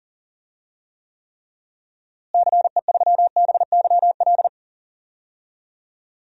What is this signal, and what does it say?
Morse code, heard as a keyed tone, about 40 wpm, 700 Hz. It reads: CE3BYL
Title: DXCC-100 Call Signs - 6 Characters - Repeated 3x - 40wpm